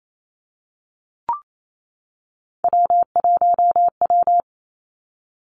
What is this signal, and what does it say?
Morse code, heard as a keyed tone, about 28 wpm, 700 Hz. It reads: W1W
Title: Three Challenging Letter-Number Combinations 28wpm